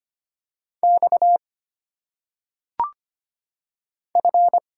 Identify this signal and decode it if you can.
Morse code, heard as a keyed tone, about 25 wpm, 700 Hz. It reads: X F
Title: Single Letters - Phonetic Pronunciation 25wpm